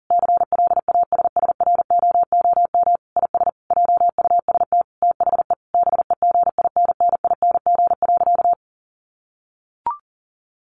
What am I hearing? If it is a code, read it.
Morse code, heard as a keyed tone, about 40 wpm, 700 Hz. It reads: CLASSROOM IS JUST THE BEGINNING.
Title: Sentences from Top 1500 Words 40wpm